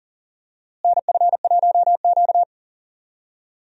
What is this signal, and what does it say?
Morse code, heard as a keyed tone, about 40 wpm, 700 Hz. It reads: NF1Q
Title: USA Call Signs - 4 Characters - Repeated 3x - 40wpm